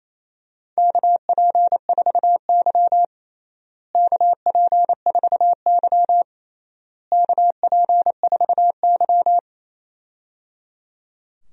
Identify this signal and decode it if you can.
Morse code, heard as a keyed tone, about 28 wpm, 700 Hz. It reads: KP4Y KP4Y KP4Y
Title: DXCC-100 Call Signs - 4 Characters - Repeated 3x - 28wpm